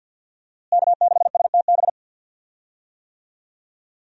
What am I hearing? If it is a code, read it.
Morse code, heard as a keyed tone, about 46 wpm, 700 Hz. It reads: K6STB
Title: USA Call Signs - 5 Characters - Repeated 3x - 50wpm